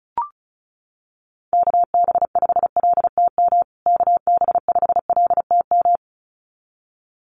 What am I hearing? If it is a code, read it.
Morse code, heard as a keyed tone, about 34 wpm, 700 Hz. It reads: KB5LTM KB5LTM
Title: USA Call Signs Repeated 2x 35wpm